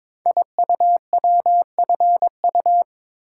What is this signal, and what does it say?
Morse code, heard as a keyed tone, about 22 wpm, 690 Hz. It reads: IUWFU
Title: Code Groups - 5 Letters - 22wpm